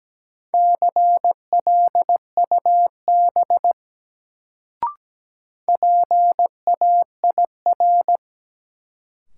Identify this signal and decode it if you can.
Morse code, heard as a keyed tone, about 17 wpm, 700 Hz. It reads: CLUB PAIR
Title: Sentences from Top 1200 Words - New Word Drill - 17wpm